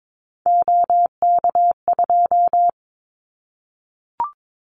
Morse code, heard as a keyed tone, about 22 wpm, 700 Hz. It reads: OK2
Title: Callsign Prefixes for Top 100 DXCC - 22wpm